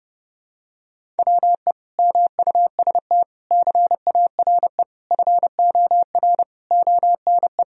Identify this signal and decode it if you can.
Morse code, heard as a keyed tone, about 30 wpm, 700 Hz. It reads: WE MUST CARE FOR ONE
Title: Sentences from Top 1000 Words - Review Part 1 - 30wpm